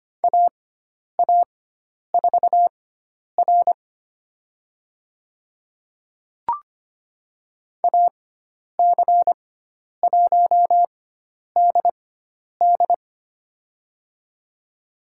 Morse code, heard as a keyed tone, about 25 wpm, spaced as for 10 wpm, 700 Hz. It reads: AA4R AC1DD
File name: Lesson 16g - Letter c - Call Signs 25wpm